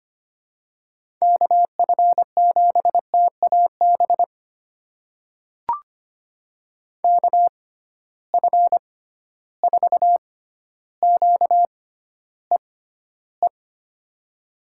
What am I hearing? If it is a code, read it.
Morse code, heard as a keyed tone, about 25 wpm, spaced as for 13 wpm, 700 Hz. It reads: KF7TAB KF4QEE
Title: USA Call Signs - 6 Characters - 6x Character Spacing - 25wpm